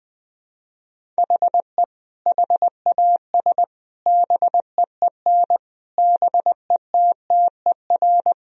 Morse code, heard as a keyed tone, about 20 wpm, 700 Hz. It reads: HE HAS BEEN BETTER
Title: Sentences from Top 1000 Words - Review Part 1 - 20wpm